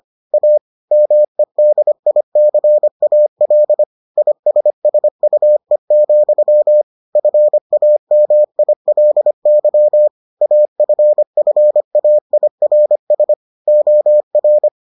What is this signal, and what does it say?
Morse code, heard as a keyed tone, about 25 wpm, 600 Hz. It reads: A MEDICAL ISSUE, FAMILY AFFAIRS OR